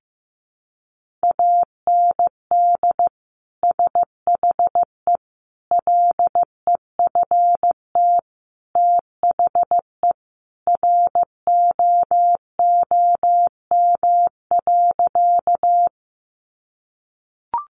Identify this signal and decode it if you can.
Morse code, heard as a keyed tone, about 15 wpm, 700 Hz. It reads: AND SHE LEFT THE ROOM.
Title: Sentences from Top 300 Words 15wpm